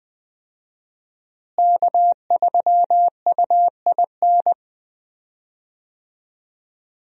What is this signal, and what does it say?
Morse code, heard as a keyed tone, about 20 wpm, 700 Hz. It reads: K3UIN
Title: USA Call Signs - 5 Characters - Repeated 3x - 20wpm